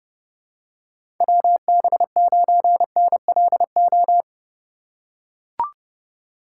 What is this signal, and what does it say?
Morse code, heard as a keyed tone, about 31 wpm, 700 Hz. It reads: WB9NLO
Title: USA Call Signs - 6 Characters - 3x Character Spacing - 30wpm